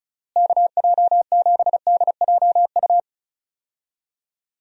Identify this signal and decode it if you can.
Morse code, heard as a keyed tone, about 36 wpm, 700 Hz. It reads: KJ7DJU
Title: USA Call Signs - 6 Characters - Repeated 3x - 35wpm